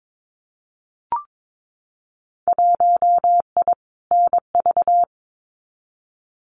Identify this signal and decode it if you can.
Morse code, heard as a keyed tone, about 22 wpm, 700 Hz. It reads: 1I NV
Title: Winter Field Day Exchanges Encore 22wpm